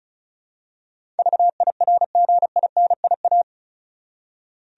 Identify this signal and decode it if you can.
Morse code, heard as a keyed tone, about 34 wpm, 700 Hz. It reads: VIRGINIA
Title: USA Common Names - 35wpm